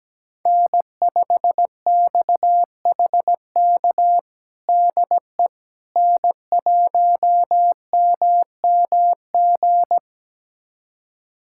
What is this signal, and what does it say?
Morse code, heard as a keyed tone, about 17 wpm, 700 Hz. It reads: N5XHK DE N1MMG
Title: USA Callsign DE Callsign - 5 Characters - 17wpm